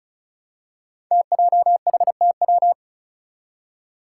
Morse code, heard as a keyed tone, about 36 wpm, 700 Hz. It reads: TJHTW
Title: Five Challenging Letter-Number Combinations 35wpm